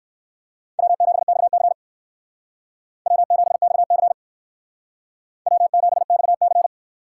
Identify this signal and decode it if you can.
Morse code, heard as a keyed tone, about 59 wpm, 700 Hz. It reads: W7XC W7XC W7XC